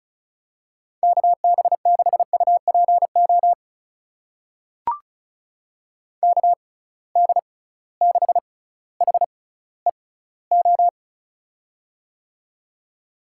Morse code, heard as a keyed tone, about 36 wpm, spaced as for 18 wpm, 700 Hz. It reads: KB6UPO KD6HEO
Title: USA Call Signs - 6 Characters - 6x Character Spacing - 35wpm